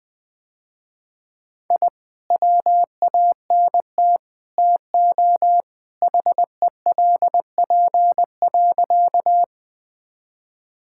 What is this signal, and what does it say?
Morse code, heard as a keyed tone, about 20 wpm, 700 Hz. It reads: I WANT TO HELP.
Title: Sentences with 3-Word Phrases - Spoken Beginning - 20wpm